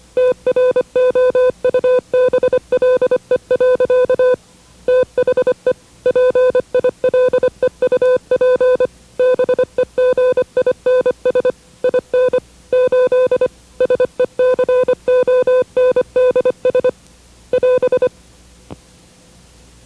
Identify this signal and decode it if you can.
Morse code, heard as a keyed tone, about 25 wpm, 500 Hz. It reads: TROUBLE. THE PILEUP BEGINS IN 8 SECONDS <AS>
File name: WRTC1990-CW-Pileup